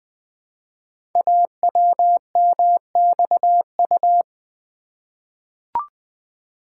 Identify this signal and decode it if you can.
Morse code, heard as a keyed tone, about 20 wpm, 700 Hz. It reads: AWMXU
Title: Code Groups - 5 Letters - 20wpm